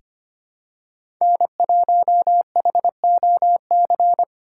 Morse code, heard as a keyed tone, about 25 wpm, 700 Hz. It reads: N1HOC
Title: USA Call Signs - 5 Characters - 4x Character Spacing - 25wpm